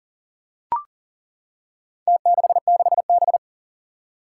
Morse code, heard as a keyed tone, about 40 wpm, 700 Hz. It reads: T66B
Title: Four Challenging Letter-Number Combinations 40wpm